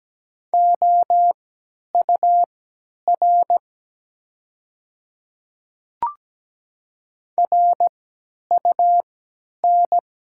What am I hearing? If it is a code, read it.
Morse code, heard as a keyed tone, about 17 wpm, spaced as for 10 wpm, 700 Hz. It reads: OUR RUN
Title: 3x Character Spacing - Top 300 Words - 17wpm